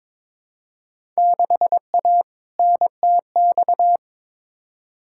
Morse code, heard as a keyed tone, about 21 wpm, 700 Hz. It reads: 6A NTX
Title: Field Day Exchanges - 22wpm